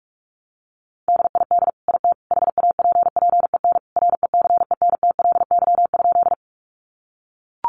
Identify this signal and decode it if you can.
Morse code, heard as a keyed tone, about 44 wpm, 700 Hz. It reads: DID IT HAPPEN RECENTLY?
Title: Sentences from Top 1200 Words 45wpm